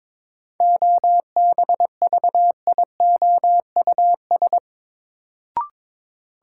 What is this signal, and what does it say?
Morse code, heard as a keyed tone, about 22 wpm, 700 Hz. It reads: OBVIOUS
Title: Sentences from Top 1600 Words - New Word Drill - 22wpm